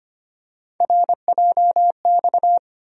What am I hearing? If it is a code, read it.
Morse code, heard as a keyed tone, about 25 wpm, 700 Hz. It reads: RJX
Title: USA Call Sign Postfixes - 3 Letters 25wpm